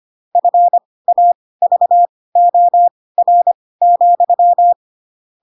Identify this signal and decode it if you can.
Morse code, heard as a keyed tone, about 25 wpm, spaced as for 18 wpm, 700 Hz. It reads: FAVOR,
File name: Pinocchio / Pinocchio ch4 12 25 ik1hge